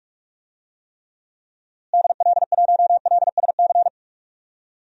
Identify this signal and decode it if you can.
Morse code, heard as a keyed tone, about 46 wpm, 700 Hz. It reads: DL1LSC
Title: DXCC-100 Call Signs - 6 Characters - Repeated 2x - 45wpm